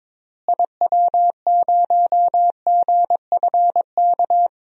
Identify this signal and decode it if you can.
Morse code, heard as a keyed tone, about 21 wpm, 700 Hz. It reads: IW0GFK
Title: DXCC-100 Call Signs - 6 Characters - Repeated 2x - 22wpm